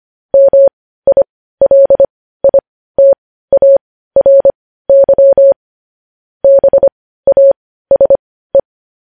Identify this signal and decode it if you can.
Morse code, heard as a keyed tone, about 25 wpm, spaced as for 15 wpm, 550 Hz. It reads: MILITARY BASE
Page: news-headlines-in-morse-code-at-15-wpm